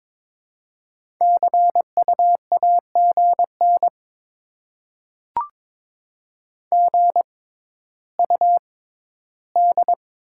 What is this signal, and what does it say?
Morse code, heard as a keyed tone, about 22 wpm, spaced as for 12 wpm, 700 Hz. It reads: CU AGN GUD
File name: Top 20 QSO Elements - 6x Character Spacing - 22wpm